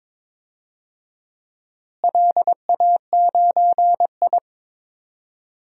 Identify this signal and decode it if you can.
Morse code, heard as a keyed tone, about 22 wpm, 700 Hz. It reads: LA9I
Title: DXCC-100 Call Signs - 4 Characters - Repeated 3x - 22wpm